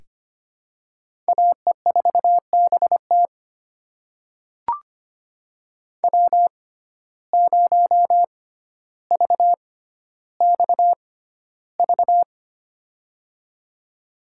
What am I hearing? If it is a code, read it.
Morse code, heard as a keyed tone, about 25 wpm, spaced as for 13 wpm, 700 Hz. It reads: AE4BT W0VXV
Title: USA Call Signs - 5 Characters - 6x Character Spacing - 25wpm